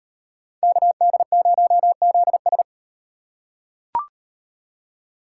Morse code, heard as a keyed tone, about 38 wpm, 700 Hz. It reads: KD0ZS